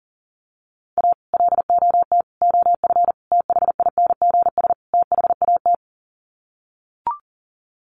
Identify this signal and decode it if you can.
Morse code, heard as a keyed tone, about 40 wpm, 700 Hz. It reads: A LOT OF THINGS THAT
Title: Sets of 5 Words 40wpm